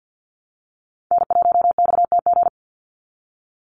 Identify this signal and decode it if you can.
Morse code, heard as a keyed tone, about 48 wpm, 700 Hz. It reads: N1XTG
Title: USA Call Signs 50wpm